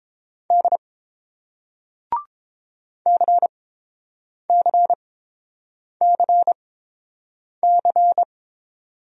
Morse code, heard as a keyed tone, about 31 wpm, spaced as for 7 wpm, 700 Hz. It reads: D CCCC